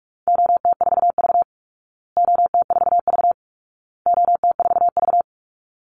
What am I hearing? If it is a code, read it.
Morse code, heard as a keyed tone, about 44 wpm, 700 Hz. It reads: OT4V OT4V OT4V
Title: Call Signs Repeated 2x - 45wpm